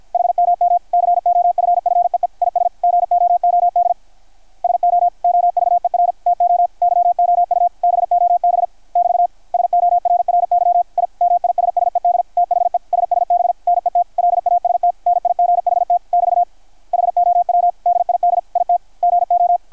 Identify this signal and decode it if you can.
Morse code, heard as a keyed tone, about 53 wpm, 660 Hz, with radio noise. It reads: HMM COFFEE IS GOOD SO OVER TO YOU BOB <BT> SORRY I MISSED THE SSB NET LAST NIGHT <BT> HOW DID IT GO
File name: archived / CRSnet